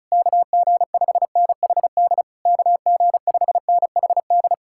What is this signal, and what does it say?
Morse code, heard as a keyed tone, about 40 wpm, 700 Hz. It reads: KG5NHD KG5NHD